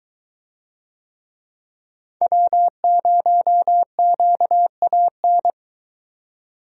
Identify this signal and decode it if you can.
Morse code, heard as a keyed tone, about 21 wpm, 700 Hz. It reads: W0QAN